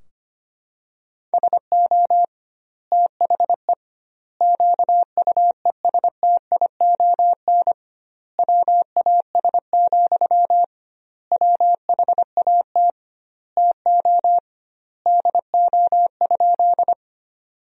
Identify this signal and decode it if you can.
Morse code, heard as a keyed tone, about 25 wpm, 700 Hz. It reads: SO THE QUESTION WAS, WHAT TO DO?
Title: 2x Word-Spacing - Sentences from Top 400 Words - 25wpm